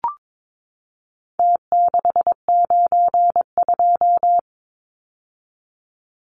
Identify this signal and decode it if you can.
Morse code, heard as a keyed tone, about 22 wpm, 700 Hz. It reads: T692
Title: Numbers - Contest Style - 4-Digit 22wpm